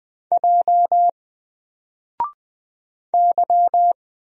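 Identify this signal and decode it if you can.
Morse code, heard as a keyed tone, about 20 wpm, 700 Hz. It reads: J Y
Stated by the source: Single letter and number Morse drill at 20 wpm, from N4LCD.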